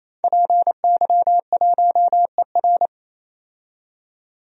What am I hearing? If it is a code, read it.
Morse code, heard as a keyed tone, about 27 wpm, 700 Hz. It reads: PY1ER
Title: DXCC-100 Call Signs - 5 Characters - 28wpm